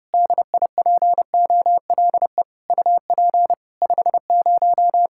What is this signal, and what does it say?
Morse code, heard as a keyed tone, about 30 wpm, 700 Hz. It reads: DIPOLE UP 50
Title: K1USN SST QSO Practice 30wpm